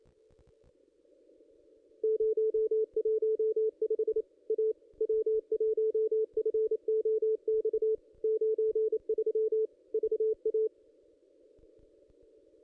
Morse code, heard as a keyed tone, about 27 wpm, 440 Hz. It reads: 015 A W1FOX 93 VA
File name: contests / ss-cw-k5tr-op-n5rz -1700